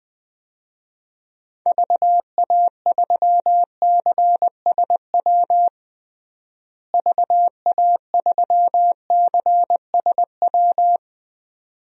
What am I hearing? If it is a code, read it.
Morse code, heard as a keyed tone, about 20 wpm, 700 Hz. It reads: VA3CSW VA3CSW
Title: DXCC-100 Call Signs - 6 Characters - Repeated 2x - 20wpm